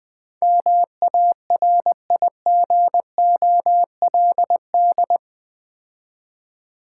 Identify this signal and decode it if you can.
Morse code, heard as a keyed tone, about 21 wpm, 690 Hz. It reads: MARIGOLD
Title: Difficult Words with Hints - 20wpm